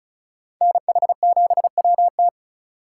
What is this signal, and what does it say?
Morse code, heard as a keyed tone, about 44 wpm, 700 Hz. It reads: NH7WT